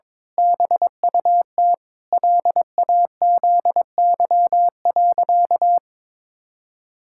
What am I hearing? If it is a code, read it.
Morse code, heard as a keyed tone, about 22 wpm, 700 Hz. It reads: BUT LAZY.
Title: Pangram Challenge v2 - 22wpm